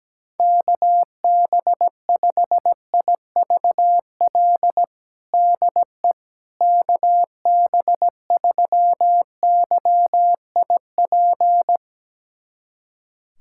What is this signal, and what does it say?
Morse code, heard as a keyed tone, about 17 wpm, 700 Hz. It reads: KB5IVL DE KB3YIP
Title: USA Callsign DE Callsign - 6 Characters - 17wpm